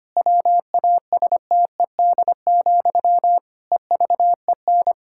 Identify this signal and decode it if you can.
Morse code, heard as a keyed tone, about 25 wpm, 700 Hz. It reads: WASTED, EVEN
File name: practice / 25wpm / Difficult Sentences with Hints - 25wpm